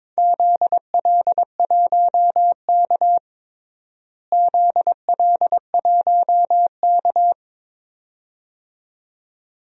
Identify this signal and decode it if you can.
Morse code, heard as a keyed tone, about 22 wpm, 700 Hz. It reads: ZL1K ZL1K
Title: DXCC-100 Call Signs - 4 Characters - Repeated 2x - 22wpm